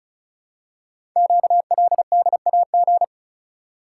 Morse code, heard as a keyed tone, about 36 wpm, 700 Hz. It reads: QLDAG
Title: Code Groups - 5 Letters - 35wpm